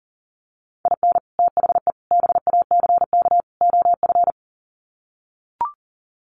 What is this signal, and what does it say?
Morse code, heard as a keyed tone, about 39 wpm, 700 Hz. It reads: IN THE BACK OF
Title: Sets of 4 Words 40wpm